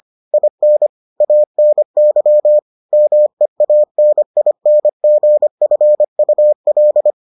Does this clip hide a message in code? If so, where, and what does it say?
Yes: Morse code from 0.3 to 7.1 s, heard as a keyed tone, about 25 wpm, 600 Hz. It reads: IN ANY MEANINGFUL